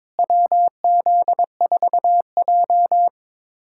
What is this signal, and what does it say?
Morse code, heard as a keyed tone, about 22 wpm, 700 Hz. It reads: WZ4J
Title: USA Callsign DE Callsign - 4 Characters - 22wpm